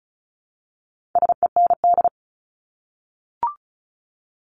Morse code, heard as a keyed tone, about 34 wpm, 700 Hz. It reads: SEND
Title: Top 1251-1500 Words 35wpm